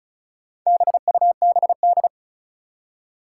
Morse code, heard as a keyed tone, about 36 wpm, 700 Hz. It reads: BUBD
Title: Four Challenging Letter-Number Combinations 35wpm